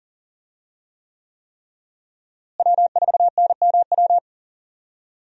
Morse code, heard as a keyed tone, about 40 wpm, 700 Hz. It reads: W4NMW
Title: USA Call Signs - 5 Characters - Repeated 3x - 40wpm